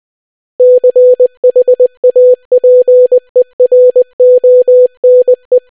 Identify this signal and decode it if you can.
Morse code, heard as a keyed tone, about 20 wpm, 500 Hz. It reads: CHAPERONE